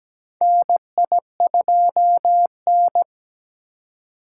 Morse code, heard as a keyed tone, about 17 wpm, 700 Hz. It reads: NI2N
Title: USA Call Signs - 4 Characters - 4x Character Spacing - 17wpm